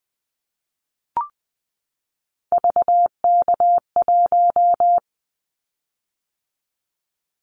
Morse code, heard as a keyed tone, about 20 wpm, 700 Hz. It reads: VK1
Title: Callsign Prefixes for Top 50 DXCC - 20wpm